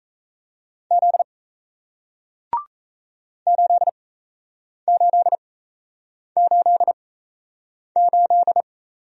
Morse code, heard as a keyed tone, about 38 wpm, spaced as for 8 wpm, 700 Hz. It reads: Z 8888